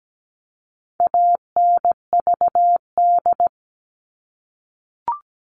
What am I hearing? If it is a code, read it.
Morse code, heard as a keyed tone, about 17 wpm, 700 Hz. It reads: ANVD
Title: All Cut Numbers - 4-Digit 17wpm